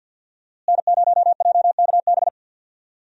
Morse code, heard as a keyed tone, about 48 wpm, 700 Hz. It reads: N0JKB
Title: USA Call Signs - 5 Characters - Repeated 2x - 50wpm